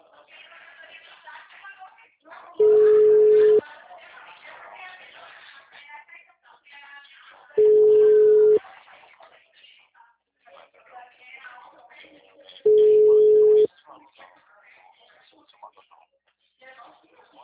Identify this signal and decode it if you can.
Ringback tone (Europe, 425 Hz), with speech.